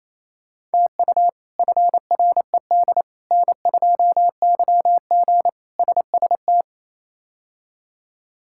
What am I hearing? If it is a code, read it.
Morse code, heard as a keyed tone, about 28 wpm, 700 Hz. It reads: TU FRED N2YG SST
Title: K1USN SST QSO Practice - 28wpm